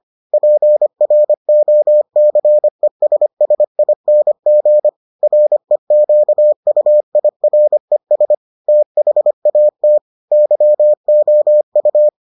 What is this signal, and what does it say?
Morse code, heard as a keyed tone, about 25 wpm, 600 Hz. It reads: PROCESSING REQUIRES THAT YOU